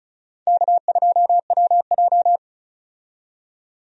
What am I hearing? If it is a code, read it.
Morse code, heard as a keyed tone, about 35 wpm, 700 Hz. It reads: K2WJ
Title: USA Call Signs - 4 Characters - 35wpm